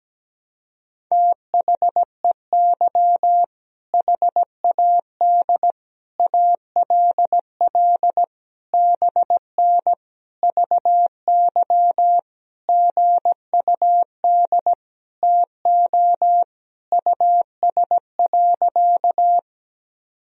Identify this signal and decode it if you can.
Morse code, heard as a keyed tone, about 17 wpm, 700 Hz. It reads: THEY HAD ALL BN VY GUD TO US.
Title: Sentences from Top 200 Words and Abbreviations 17wpm